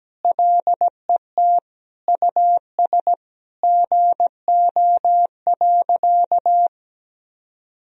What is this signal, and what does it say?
Morse code, heard as a keyed tone, about 17 wpm, 700 Hz. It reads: LET US GO.
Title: Sentences from Top 1000 Words - Review Part 2 - 17wpm